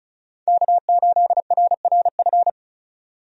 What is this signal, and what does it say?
Morse code, heard as a keyed tone, about 36 wpm, 700 Hz. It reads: K8RRF
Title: USA Call Signs - 5 Characters - 35wpm